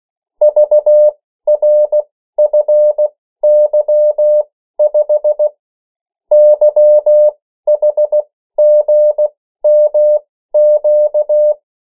Morse code, heard as a keyed tone, about 16 wpm, spaced as for 13 wpm, 600 Hz. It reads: VRFY5 YHGMQ
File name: morse mp3 / 14 urok 13